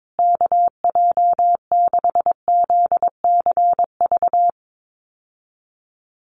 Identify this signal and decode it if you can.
Morse code, heard as a keyed tone, about 22 wpm, 700 Hz. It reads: KJ6ZCV
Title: USA Call Signs Repeated 3x 22wpm